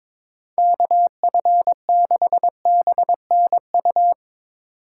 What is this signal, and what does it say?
Morse code, heard as a keyed tone, about 22 wpm, 700 Hz. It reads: KF6BNU
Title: USA Call Signs - 6 Characters - 4x Character Spacing - 22wpm